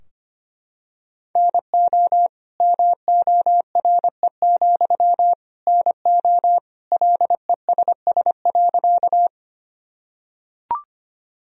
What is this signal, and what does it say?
Morse code, heard as a keyed tone, about 25 wpm, 700 Hz. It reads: NO MORE, NO LESS.
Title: Sentences from Top 1000 Words - Review Part 2 - 25wpm